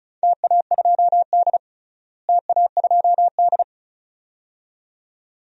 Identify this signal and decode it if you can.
Morse code, heard as a keyed tone, about 36 wpm, 700 Hz. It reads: TA2D TA2D
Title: DXCC-100 Call Signs - 4 Characters - Repeated 2x - 35wpm